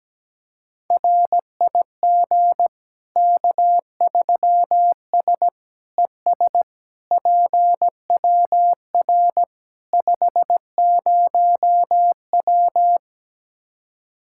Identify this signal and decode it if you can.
Morse code, heard as a keyed tone, about 17 wpm, 700 Hz. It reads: RIG K3S ES PWR 50W
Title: Practice QSO - Exchange 2 - Info - 17wpm